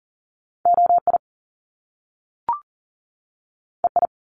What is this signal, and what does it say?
Morse code, heard as a keyed tone, about 40 wpm, 700 Hz. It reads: OI EI
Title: Top Two Letter Combinations in Words 40wpm